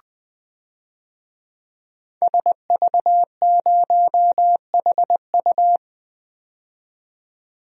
Morse code, heard as a keyed tone, about 20 wpm, 700 Hz. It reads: SV0HU
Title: DXCC-100 Call Signs - 5 Characters - 20wpm